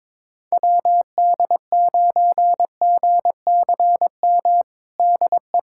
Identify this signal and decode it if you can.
Morse code, heard as a keyed tone, about 21 wpm, 700 Hz. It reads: WD9GCM DE
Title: USA Callsign DE Callsign - 6 Characters - 22wpm